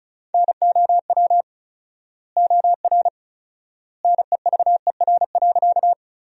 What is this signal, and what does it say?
Morse code, heard as a keyed tone, about 34 wpm, 700 Hz. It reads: NOW OR NEVER.
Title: 4x Word-Spacing - Sentences from Top 300 Words - 35wpm